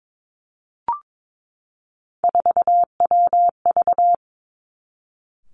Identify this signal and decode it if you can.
Morse code, heard as a keyed tone, about 22 wpm, 700 Hz. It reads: 4WV
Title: Three Challenging Letter-Number Combinations 22wpm